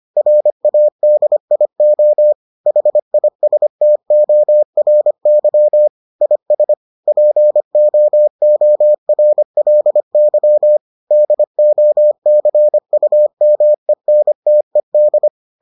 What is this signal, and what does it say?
Morse code, heard as a keyed tone, about 25 wpm, 600 Hz. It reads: RADIO HISTORY IS POORLY DOCUMENTED